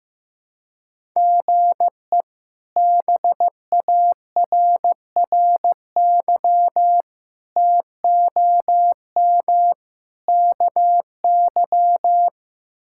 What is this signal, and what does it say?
Morse code, heard as a keyed tone, about 15 wpm, 700 Hz. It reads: GE BARRY TOM KY
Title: K1USN SST QSO Practice - 15wpm